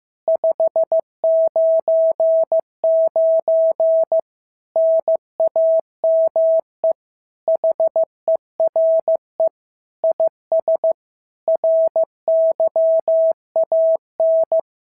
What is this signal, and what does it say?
Morse code, heard as a keyed tone, about 14 wpm, 650 Hz. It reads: 599 NAME HERE IS RYAN
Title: SOTA QSO Practice 15wpm